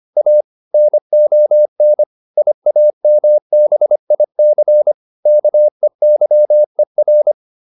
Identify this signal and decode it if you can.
Morse code, heard as a keyed tone, about 25 wpm, 600 Hz. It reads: A NON IAMBIC KEYER